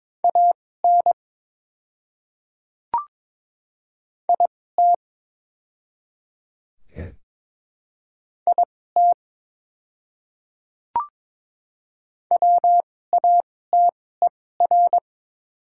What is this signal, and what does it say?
Morse code, heard as a keyed tone, about 22 wpm, spaced as for 16 wpm, 700 Hz. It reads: AN IT IT WATER
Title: 2x Character Spacing - Top 100 Words - 22wpm